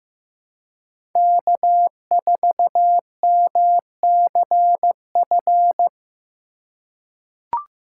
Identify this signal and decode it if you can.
Morse code, heard as a keyed tone, about 15 wpm, 700 Hz. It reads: K4MCF
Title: USA Call Signs - 5 Characters - 3x Character Spacing - 15wpm